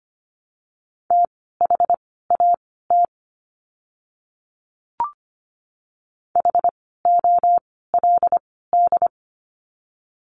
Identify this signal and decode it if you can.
Morse code, heard as a keyed tone, about 25 wpm, spaced as for 16 wpm, 700 Hz. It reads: THAT HOLD
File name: course / 25:10wpm / 094 Lesson 16f - Letter c - Words 25wpm